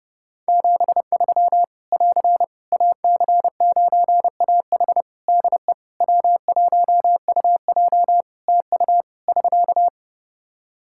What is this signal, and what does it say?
Morse code, heard as a keyed tone, about 30 wpm, 700 Hz. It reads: 73 <AR> AC9AH DE W1UJ TU <SK>
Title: K1USN SST QSO Practice 30wpm